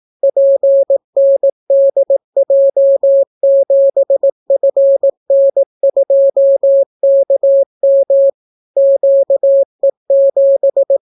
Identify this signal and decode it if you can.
Morse code, heard as a keyed tone, about 18 wpm, 550 Hz. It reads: PNDJ7FN2KM QE7